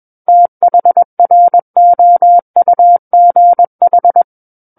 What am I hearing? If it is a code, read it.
Morse code, heard as a keyed tone, about 21 wpm, 700 Hz. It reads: THROUGH
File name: Book One / 12-Chapter 12 - 21 WPM